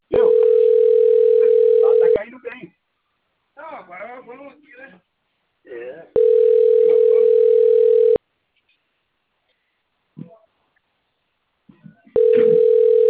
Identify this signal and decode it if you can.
Ringback tone (North America), 440 + 480 Hz, with speech.
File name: out- - - - - .54741